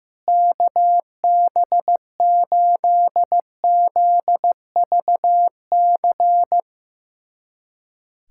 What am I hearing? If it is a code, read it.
Morse code, heard as a keyed tone, about 15 wpm, 700 Hz. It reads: KB8ZVC
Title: USA Callsign DE Callsign - 6 Characters - 15wpm